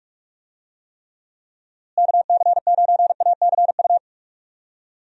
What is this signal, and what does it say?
Morse code, heard as a keyed tone, about 46 wpm, 700 Hz. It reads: KC9ACU
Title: USA Call Signs - 6 Characters - Repeated 3x - 45wpm